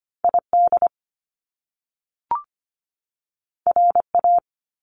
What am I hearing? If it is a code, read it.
Morse code, heard as a keyed tone, about 25 wpm, 700 Hz. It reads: ID RA
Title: Top Two Letter Combinations in Words 25wpm